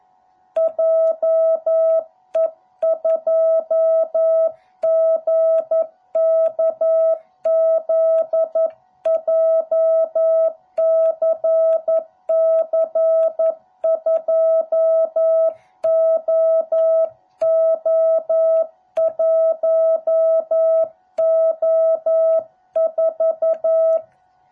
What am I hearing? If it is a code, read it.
Morse code, heard as a keyed tone, about 11 wpm, 660 Hz. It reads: JE2GKZ JCC2OO1O4